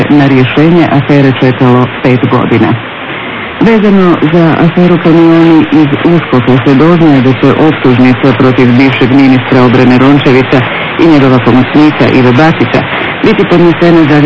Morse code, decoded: none